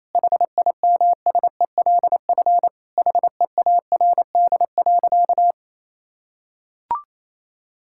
Morse code, heard as a keyed tone, about 28 wpm, 700 Hz. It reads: HIMSELF HEARD.
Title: Sentences from Top 1600 Words 28wpm